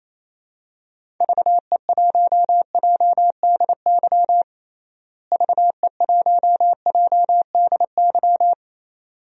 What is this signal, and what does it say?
Morse code, heard as a keyed tone, about 27 wpm, 700 Hz. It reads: VE1JDY VE1JDY
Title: DXCC-100 Call Signs - 6 Characters - Repeated 2x - 28wpm